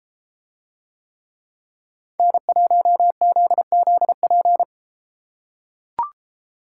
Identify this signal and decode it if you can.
Morse code, heard as a keyed tone, about 32 wpm, 700 Hz. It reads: N1ZZP